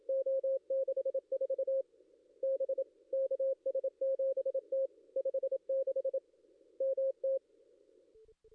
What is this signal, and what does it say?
Morse code, heard as a keyed tone, about 27 wpm, 540 Hz. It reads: O64 B KS7T 56 MT
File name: contests / ss-cw-k5tr-op-n5rz -2300